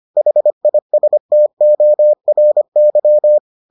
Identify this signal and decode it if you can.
Morse code, heard as a keyed tone, about 25 wpm, 600 Hz. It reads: HISTORY